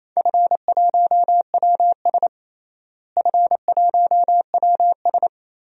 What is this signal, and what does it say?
Morse code, heard as a keyed tone, about 28 wpm, 700 Hz. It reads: F1WS F1WS
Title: DXCC-100 Call Signs - 4 Characters - Repeated 2x - 28wpm